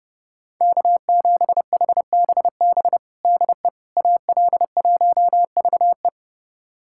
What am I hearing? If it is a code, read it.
Morse code, heard as a keyed tone, about 30 wpm, 700 Hz. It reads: K7HBB DE AL1VE
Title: USA Callsign DE Callsign - 5 Characters - 30wpm